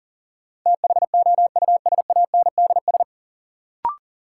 Morse code, heard as a keyed tone, about 40 wpm, 700 Hz. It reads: THOUSANDS
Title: Sets of 4 Words - New Word Drill - 40wpm